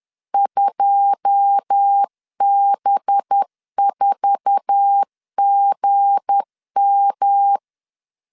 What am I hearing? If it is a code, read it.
Morse code, heard as a keyed tone, about 10 wpm, 800 Hz. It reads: 2B4GM
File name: LEZIONI / 10 SIGNS / 10B INT 10wpm 2844